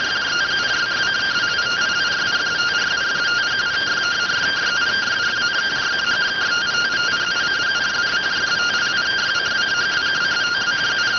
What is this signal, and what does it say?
Two-tone teleprinter signal, about 170 Hz shift.